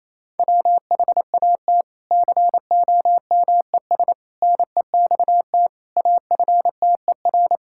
Morse code, heard as a keyed tone, about 27 wpm, 700 Hz. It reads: WHAT COMES NEXT AFTER
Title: Sentences from Top 1000 Words - Review Part 5 - 28wpm